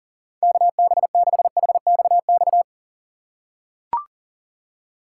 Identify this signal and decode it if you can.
Morse code, heard as a keyed tone, about 37 wpm, 700 Hz. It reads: KB6HXX
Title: USA Call Signs - 6 Characters - 6x Character Spacing - 40wpm